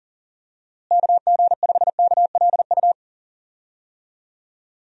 Morse code, heard as a keyed tone, about 40 wpm, 700 Hz. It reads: KG5KLU
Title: USA Call Signs - 6 Characters - Repeated 2x - 40wpm